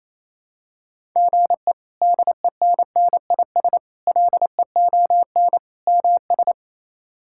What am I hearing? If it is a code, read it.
Morse code, heard as a keyed tone, about 27 wpm, 700 Hz. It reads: GE DENNIS LEON MS
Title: K1USN SST QSO Practice - 28wpm